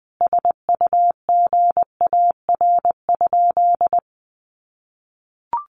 Morse code, heard as a keyed tone, about 20 wpm, 700 Hz. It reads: SUGAR?
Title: Sentences from Top 1200 Words 20wpm